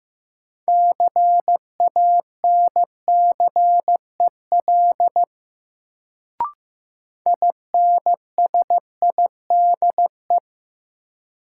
Morse code, heard as a keyed tone, about 16 wpm, 700 Hz. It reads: CANCEL INSIDE
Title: Sentences from Top 1200 Words - New Word Drill Alt - 15wpm